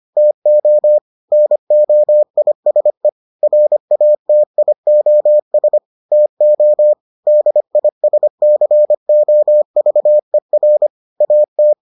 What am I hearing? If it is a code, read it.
Morse code, heard as a keyed tone, about 25 wpm, 600 Hz. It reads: TO NOISE RATIOS TO DISCOVER AT